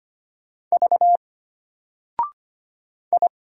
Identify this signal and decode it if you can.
Morse code, heard as a keyed tone, about 25 wpm, 700 Hz. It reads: V I
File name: Single-Letter-Number-25wpm